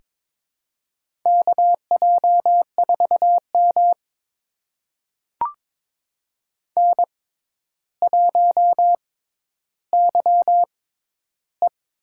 Morse code, heard as a keyed tone, about 22 wpm, spaced as for 11 wpm, 700 Hz. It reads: KJ4M N1YE
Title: USA Call Signs - 4 Characters - 6x Character Spacing - 22wpm